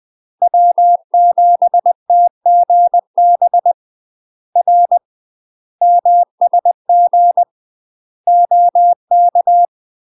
Morse code, heard as a keyed tone, about 20 wpm, 700 Hz. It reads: W7TGB R MSG OK
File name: UK MASTER QSOs TRAINING / 20WPM / UK MASTER QSOs 0020